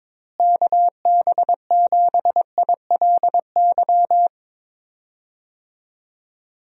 Morse code, heard as a keyed tone, about 22 wpm, 700 Hz. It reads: KB7ILY
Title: USA Call Signs - 6 Characters - Repeated 3x - 22wpm